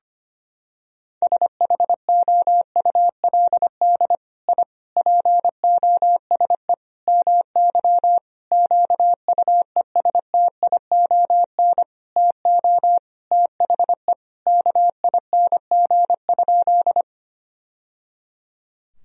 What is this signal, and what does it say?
Morse code, heard as a keyed tone, about 25 wpm, 700 Hz. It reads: SHOULD I POSE MY QUESTION TO THE KING?
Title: Sentences from Top 1000 Words - Review Part 2 - 25wpm